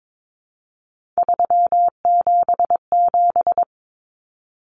Morse code, heard as a keyed tone, about 22 wpm, 700 Hz. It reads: 377
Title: Numbers - 3-Digit 22wpm